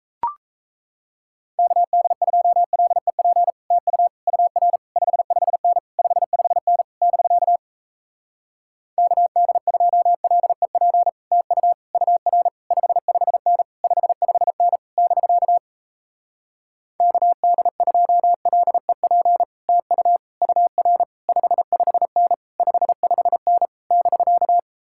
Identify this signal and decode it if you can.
Morse code, heard as a keyed tone, about 37 wpm, 700 Hz. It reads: KD2LEP TU UR 55N 55N <BK> KD2LEP TU UR 55N 55N <BK> KD2LEP TU UR 55N 55N <BK>